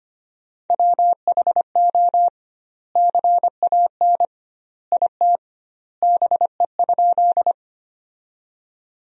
Morse code, heard as a keyed tone, about 25 wpm, 700 Hz. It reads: WHO CAN IT BE?
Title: 2x Word-Spacing - Sentences from Top 100 Words - 25wpm